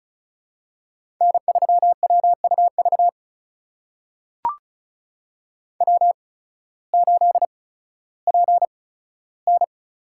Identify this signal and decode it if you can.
Morse code, heard as a keyed tone, about 36 wpm, spaced as for 16 wpm, 700 Hz. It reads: N3WUV W8PN
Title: USA Call Signs - 5 Characters - 8x Character Spacing - 35wpm